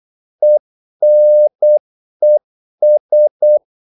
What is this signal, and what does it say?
Morse code, heard as a keyed tone, about 8 wpm, 600 Hz. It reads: ENES
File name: Korte ord DK 8wpm